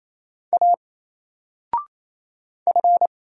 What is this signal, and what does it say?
Morse code, heard as a keyed tone, about 28 wpm, 700 Hz. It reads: A F
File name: Single-Letter-Number-28wpm